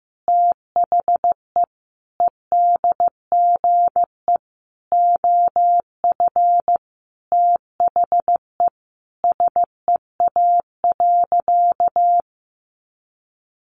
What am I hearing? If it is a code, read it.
Morse code, heard as a keyed tone, about 15 wpm, 700 Hz. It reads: THE EDGE OF THE SEA.
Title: Sentences from Top 1300 Words 15wpm